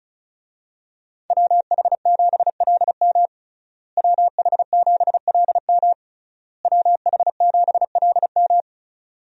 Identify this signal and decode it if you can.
Morse code, heard as a keyed tone, about 34 wpm, 700 Hz. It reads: WH7LM WH7LM WH7LM
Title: USA Call Signs - 5 Characters - Repeated 3x - 35wpm